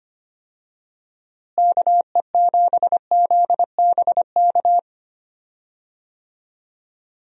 Morse code, heard as a keyed tone, about 25 wpm, 700 Hz. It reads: KE7ZBK
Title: USA Call Signs - 6 Characters - Repeated 3x - 25wpm